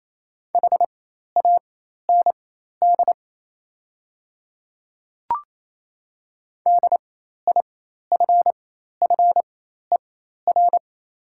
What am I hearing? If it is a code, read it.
Morse code, heard as a keyed tone, about 28 wpm, spaced as for 13 wpm, 700 Hz. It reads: HAND DIFFER
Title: 4x Character Spacing - Top 200 Words - 28wpm